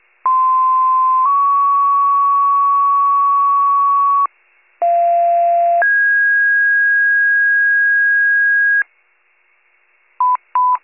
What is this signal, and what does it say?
Two-tone page, heard as a one-second tone, then a three-second tone, about 1040 then 1120 Hz, about 690 then 1650 Hz, with noise. It is heard from 0.3 to 8.8 s.
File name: pager / 73 RESCUE / wcalert